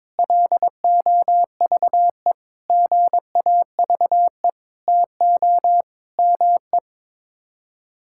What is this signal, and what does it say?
Morse code, heard as a keyed tone, about 22 wpm, 700 Hz. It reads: LOVE GAVE TO ME
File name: Sets of 9 Words - 22wpm